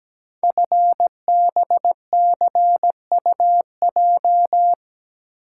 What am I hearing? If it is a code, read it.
Morse code, heard as a keyed tone, about 17 wpm, 700 Hz. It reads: FBCUJ
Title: Code Groups - 5 Letters - 17wpm